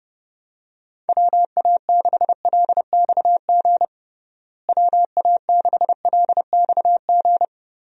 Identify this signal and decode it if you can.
Morse code, heard as a keyed tone, about 32 wpm, 700 Hz. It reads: WA6LXG WA6LXG
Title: USA Call Signs - 6 Characters - Repeated 3x - 30wpm